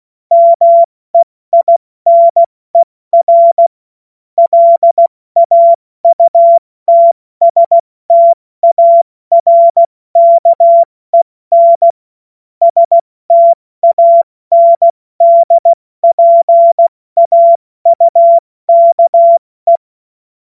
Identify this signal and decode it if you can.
Morse code, heard as a keyed tone, about 16 wpm, 680 Hz. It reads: MEINER LAUTSTARKEN STANDPAUKE
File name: Morsekurs / Witze / Witz 29. Juni 2021, 70 F80